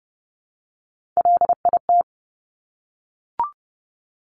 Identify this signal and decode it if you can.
Morse code, heard as a keyed tone, about 30 wpm, 700 Hz. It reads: LIT
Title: Top Three Letter Combinations in Words 30wpm